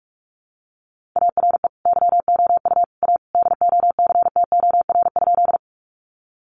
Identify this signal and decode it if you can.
Morse code, heard as a keyed tone, about 44 wpm, 700 Hz. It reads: ARE YOU A DOCTOR?
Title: Sentences from Top 1500 Words 45wpm